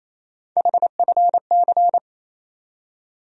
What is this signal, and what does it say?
Morse code, heard as a keyed tone, about 28 wpm, 700 Hz. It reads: HFC
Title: USA Call Sign Postfixes - 3 Letters 28wpm